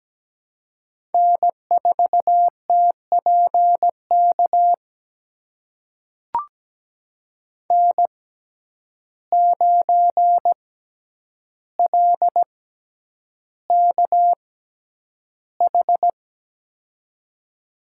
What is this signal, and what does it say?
Morse code, heard as a keyed tone, about 17 wpm, spaced as for 9 wpm, 700 Hz. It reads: N4TPK N9LKH
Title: USA Call Signs - 5 Characters - 6x Character Spacing - 17wpm